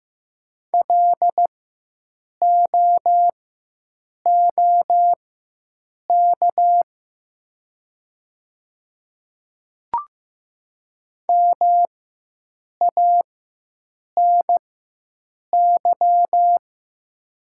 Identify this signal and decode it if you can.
Morse code, heard as a keyed tone, about 15 wpm, spaced as for 7 wpm, 700 Hz. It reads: LOOK MANY
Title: 4x Character Spacing - Top 100 Words - 15wpm